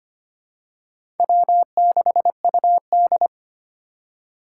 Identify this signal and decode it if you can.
Morse code, heard as a keyed tone, about 25 wpm, 700 Hz. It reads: W6UD
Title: USA Call Signs - 4 Characters - 6x Character Spacing - 25wpm